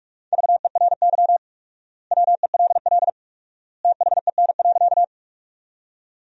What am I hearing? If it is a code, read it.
Morse code, heard as a keyed tone, about 43 wpm, 700 Hz. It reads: VERY WELL THEN.
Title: 4x Word-Spacing - Sentences from Top 200 Words - 45wpm